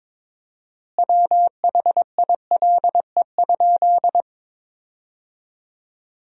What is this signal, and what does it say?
Morse code, heard as a keyed tone, about 22 wpm, 700 Hz. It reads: WHILE?
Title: 4x Word-Spacing - Sentences from Top 300 Words - 22wpm